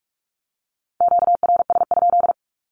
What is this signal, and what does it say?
Morse code, heard as a keyed tone, about 44 wpm, 700 Hz. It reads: QRS?
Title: Q-Codes 45wpm